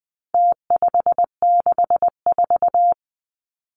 Morse code, heard as a keyed tone, about 20 wpm, 700 Hz. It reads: T564
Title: Numbers - Contest Style - 4-Digit 20wpm